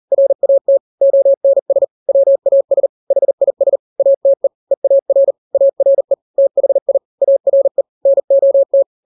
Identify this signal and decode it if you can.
Morse code, heard as a keyed tone, about 40 wpm, 550 Hz. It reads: RAT ONS WAS HIS ATE EAR ARE THI ARE NOT
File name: n-grams / 3-grams 40 35 WPM